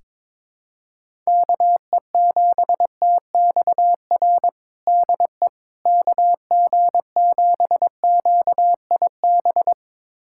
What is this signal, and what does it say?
Morse code, heard as a keyed tone, about 22 wpm, 700 Hz. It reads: KE7TXR DE KG7QIB
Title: USA Callsign DE Callsign - 6 Characters - 22wpm